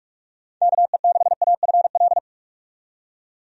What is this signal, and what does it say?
Morse code, heard as a keyed tone, about 45 wpm, 700 Hz. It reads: KE6AFL
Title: USA Call Signs - 6 Characters - 45wpm